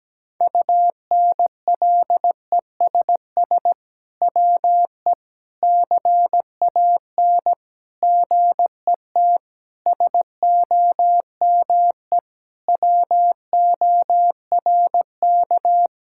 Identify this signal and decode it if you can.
Morse code, heard as a keyed tone, about 17 wpm, 700 Hz. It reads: UNLESS WE CAN GET SOME WORK
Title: Sentences from Top 1000 Words - Review Part 6 - 17wpm